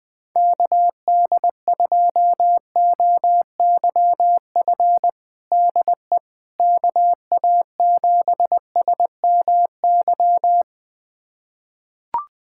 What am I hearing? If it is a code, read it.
Morse code, heard as a keyed tone, about 20 wpm, 700 Hz. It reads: KD2OYF DE KA7SMY
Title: USA Callsign DE Callsign - 6 Characters - 20wpm